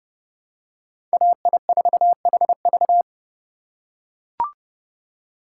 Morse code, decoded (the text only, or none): AI4HV